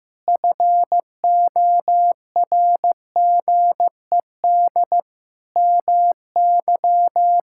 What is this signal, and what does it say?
Morse code, heard as a keyed tone, about 15 wpm, 700 Hz. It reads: FORGED MY
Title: Pangram Challenge v2 - 15wpm